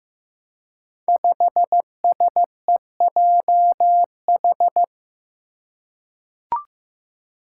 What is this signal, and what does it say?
Morse code, heard as a keyed tone, about 15 wpm, 700 Hz. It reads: 5SEJH
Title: Five Challenging Letter-Number Combinations 15wpm